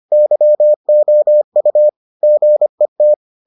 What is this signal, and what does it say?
Morse code, heard as a keyed tone, about 25 wpm, 600 Hz. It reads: YOU GET